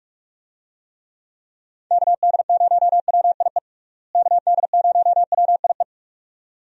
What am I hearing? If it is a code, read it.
Morse code, heard as a keyed tone, about 46 wpm, 700 Hz. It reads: KD0WIE KD0WIE
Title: USA Call Signs - 6 Characters - Repeated 2x - 45wpm